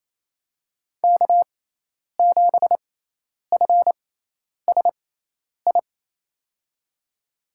Morse code, heard as a keyed tone, about 28 wpm, spaced as for 10 wpm, 700 Hz. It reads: K7FSI
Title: USA Call Signs - 5 Characters - 6x Character Spacing - 28wpm